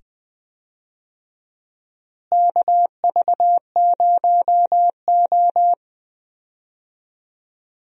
Morse code, heard as a keyed tone, about 20 wpm, 700 Hz. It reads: KV0O
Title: USA Call Signs - 4 Characters - Repeated 2x - 20wpm